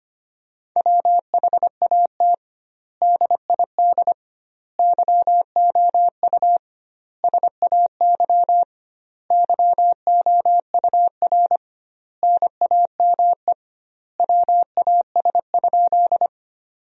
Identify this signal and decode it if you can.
Morse code, heard as a keyed tone, about 25 wpm, 700 Hz. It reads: WHAT DID YOU SAY YOUR NAME WAS?
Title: 2x Word-Spacing - Sentences from Top 200 Words - 25wpm